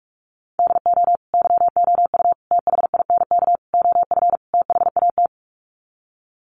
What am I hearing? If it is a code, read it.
Morse code, heard as a keyed tone, about 44 wpm, 700 Hz. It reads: DO YOU THINK OF THAT
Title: Sets of 5 Words Encore2 45wpm